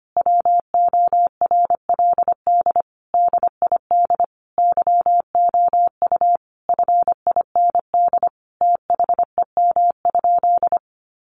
Morse code, heard as a keyed tone, about 25 wpm, 700 Hz. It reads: WORLD DID YOU FIND THEM?
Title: Sentences from Top 200 Words 25wpm